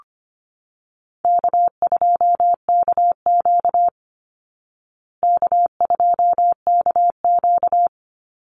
Morse code, heard as a keyed tone, about 25 wpm, 700 Hz. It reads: K2KQ K2KQ
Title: Call Signs Repeated 3x - 25wpm